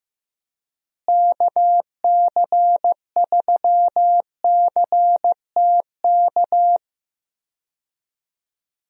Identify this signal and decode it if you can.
Morse code, heard as a keyed tone, about 15 wpm, 700 Hz. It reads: KC3CTK
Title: USA Call Signs - 6 Characters - Repeated 3x - 15wpm